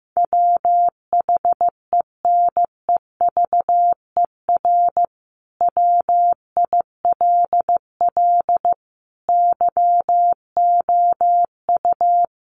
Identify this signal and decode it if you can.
Morse code, heard as a keyed tone, about 15 wpm, 700 Hz. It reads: WHENEVER WILL YOU
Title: Sentences from Top 1200 Words 15wpm